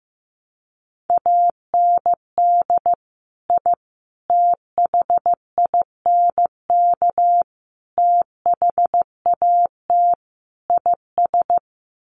Morse code, heard as a keyed tone, about 15 wpm, 700 Hz. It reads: AND I THINK THAT IS
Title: Sets of 5 Words Encore2 15wpm